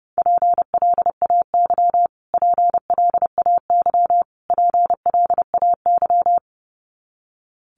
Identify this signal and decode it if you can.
Morse code, heard as a keyed tone, about 30 wpm, 700 Hz. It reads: PLAY PLAY PLAY
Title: Top 500 Words Repeated 30wpm